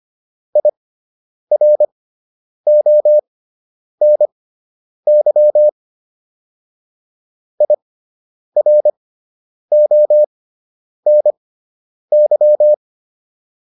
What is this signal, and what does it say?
Morse code, heard as a keyed tone, about 25 wpm, spaced as for 9 wpm, 600 Hz. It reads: IRONY IRONY